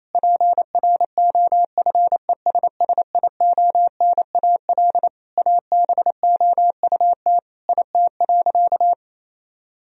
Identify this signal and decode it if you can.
Morse code, heard as a keyed tone, about 28 wpm, 700 Hz. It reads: PROFESSIONAL ABOUT IT.
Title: Sentences from Top 1000 Words - Review Part 4 - 28wpm